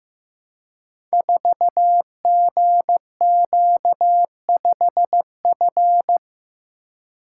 Morse code, heard as a keyed tone, about 15 wpm, 700 Hz. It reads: 4GQ5F
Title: Code Groups - 5 Letter:Numbers - 15wpm